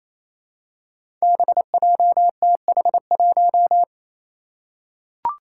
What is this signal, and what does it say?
Morse code, heard as a keyed tone, about 29 wpm, 700 Hz. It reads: BJTH1
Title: Five Challenging Letter-Number Combinations 28wpm